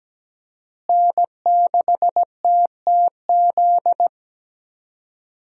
Morse code, heard as a keyed tone, about 17 wpm, 700 Hz. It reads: N6TTZ
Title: USA Call Signs - 5 Characters - Repeated 3x - 17wpm